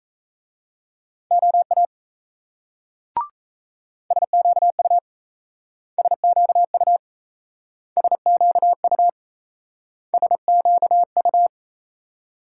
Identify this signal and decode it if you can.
Morse code, heard as a keyed tone, about 38 wpm, 700 Hz. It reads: OA SQU SQU SQU SQU